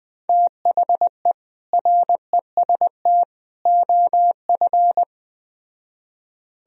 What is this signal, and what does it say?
Morse code, heard as a keyed tone, about 20 wpm, 700 Hz. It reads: THE REST OF